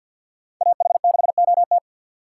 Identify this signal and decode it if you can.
Morse code, heard as a keyed tone, about 61 wpm, 700 Hz. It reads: AH6OT
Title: Speed Racing - USA 2x2 Call Signs 50wpm